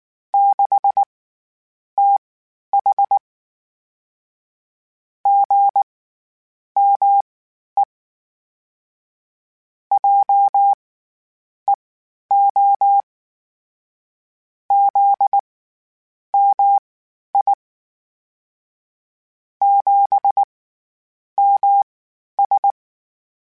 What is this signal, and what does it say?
Morse code, heard as a keyed tone, about 19 wpm, spaced as for 9 wpm, 800 Hz. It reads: 6TH GME JEO ZMI 7MS
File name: CW Comparisons / CW Comparisons-mp3s / CW Comparisons Broken Characters @18